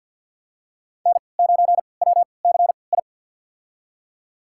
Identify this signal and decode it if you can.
Morse code, heard as a keyed tone, about 49 wpm, spaced as for 28 wpm, 690 Hz. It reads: N9WCI
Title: USA Call Signs - 5 Characters - 3x Character Spacing - 50wpm